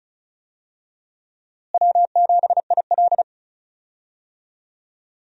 Morse code, heard as a keyed tone, about 34 wpm, 700 Hz. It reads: W7IL
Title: USA Call Signs - 4 Characters - Repeated 2x - 35wpm